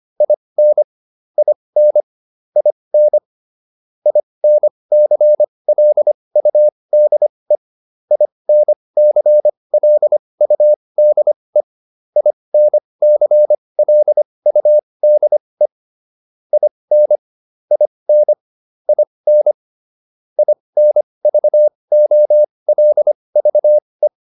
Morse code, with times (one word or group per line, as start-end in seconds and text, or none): 0.2-0.8 IN
1.4-2.0 IN
2.6-3.2 IN
4.1-7.6 INCLUDE
8.1-11.6 INCLUDE
12.2-15.7 INCLUDE
16.5-17.2 IN
17.7-18.3 IN
18.9-19.5 IN
20.4-24.1 INVOLVE